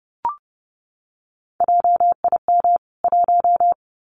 Morse code, heard as a keyed tone, about 30 wpm, 700 Hz. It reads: JIM 1
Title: CWT Exchange 30wpm